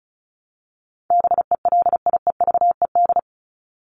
Morse code, heard as a keyed tone, about 34 wpm, 700 Hz. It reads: BELIEVED
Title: ED Suffix - Top 100 Words with Base Words 35wpm